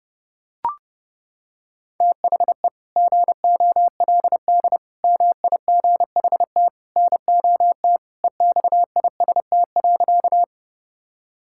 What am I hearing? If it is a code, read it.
Morse code, heard as a keyed tone, about 30 wpm, 700 Hz. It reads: THE GOLD MIGHT NOT EXIST.
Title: Sentences from Top 1000 Words - Review Part 5 - 30wpm